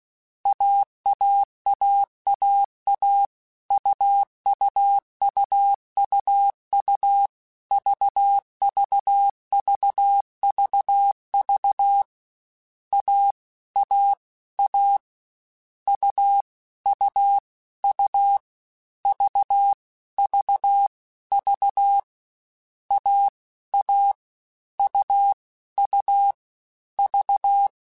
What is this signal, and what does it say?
Morse code, heard as a keyed tone, about 16 wpm, 800 Hz. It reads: AAAAA UUUUU VVVVV A A A U U U V V V A A U U V
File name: Difficult Combos / Difficult Combos-mp3s / AUV AUV4 @15